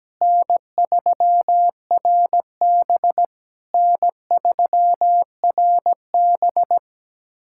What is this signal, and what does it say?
Morse code, heard as a keyed tone, about 17 wpm, 700 Hz. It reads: N3RB N3RB
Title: SOTA QSO Practice 17wpm